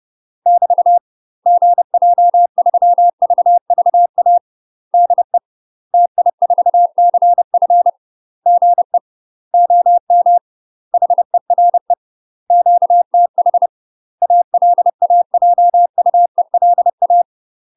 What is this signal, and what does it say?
Morse code, heard as a keyed tone, about 30 wpm, 700 Hz. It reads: <BT> GJ3VVA DE TI4CF GE OM HERE QTH ALAJUELA
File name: UK MASTER QSOs TRAINING / 30WPM / UK MASTER QSOs 0023